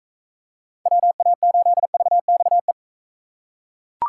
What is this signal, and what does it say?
Morse code, heard as a keyed tone, about 42 wpm, 700 Hz. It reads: WA8VXE